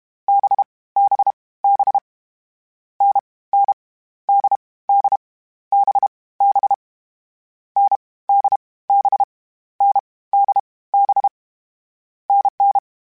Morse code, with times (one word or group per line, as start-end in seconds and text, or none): 0.3-0.6 B
1.0-1.3 B
1.6-2.0 B
3.0-3.2 N
3.5-3.7 N
4.3-4.6 D
4.9-5.2 D
5.7-6.1 B
6.4-6.7 B
7.8-8.0 N
8.3-8.6 D
8.9-9.2 B
9.8-10.0 N
10.3-10.6 D
10.9-11.3 B
12.3-12.8 NN